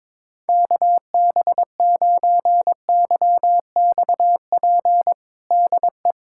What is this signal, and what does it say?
Morse code, heard as a keyed tone, about 22 wpm, 700 Hz. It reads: KB9YXP DE
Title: USA Callsign DE Callsign 22wpm